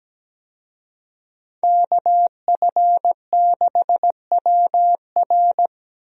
Morse code, heard as a keyed tone, about 17 wpm, 700 Hz. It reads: KF6WR
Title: USA Call Signs - 5 Characters - Repeated 2x - 17wpm